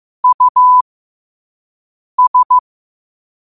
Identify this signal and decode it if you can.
Morse code, heard as a keyed tone, about 15 wpm, spaced as for 5 wpm, 1000 Hz. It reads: US
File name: qotdfiles / 5wpm